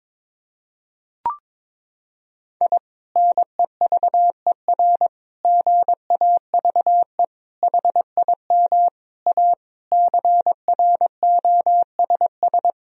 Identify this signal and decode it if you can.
Morse code, heard as a keyed tone, about 22 wpm, 700 Hz. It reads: I NEVER GAVE HIM A CROSS
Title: Sentences from Top 1000 Words - Review Part 1 - 22wpm